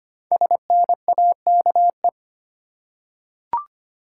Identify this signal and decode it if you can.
Morse code, heard as a keyed tone, about 25 wpm, 700 Hz. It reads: SNAKE
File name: 5 Letter Words - First 3 Letters Given - 25wpm